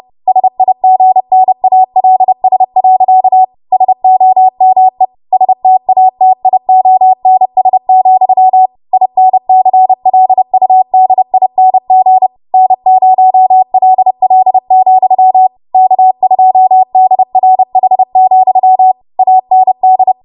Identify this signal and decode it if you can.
Morse code, heard as a keyed tone, about 30 wpm, 750 Hz. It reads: SIGNALS. SOME STATIONS, INCLUDING N0LL, K2DRH, AND
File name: Morse / 30WPM